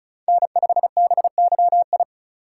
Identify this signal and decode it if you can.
Morse code, heard as a keyed tone, about 34 wpm, 700 Hz. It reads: N5BYI
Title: USA Call Signs - 4x Character Spacing - 35wpm